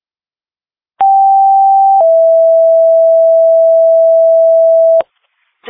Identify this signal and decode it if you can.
Two-tone page, heard as a one-second tone, then a three-second tone, about 780 then 660 Hz. It is heard from 1.0 to 5.0 s.